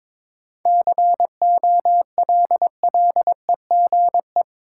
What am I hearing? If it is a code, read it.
Morse code, heard as a keyed tone, about 21 wpm, 700 Hz. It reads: COLLEGE
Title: Sentences from Top 1300 Words - New Word Drill Alt - 22wpm